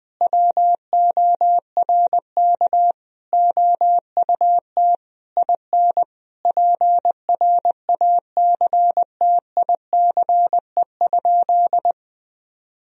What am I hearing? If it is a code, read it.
Morse code, heard as a keyed tone, about 21 wpm, 700 Hz. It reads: WORK OUT IN PRACTICE?
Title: Sentences from Top 1000 Words - Review Part 4 - 20wpm